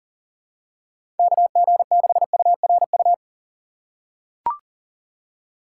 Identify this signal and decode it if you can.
Morse code, heard as a keyed tone, about 37 wpm, 700 Hz. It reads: KG6URU
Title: USA Call Signs - 4x Character Spacing - 40wpm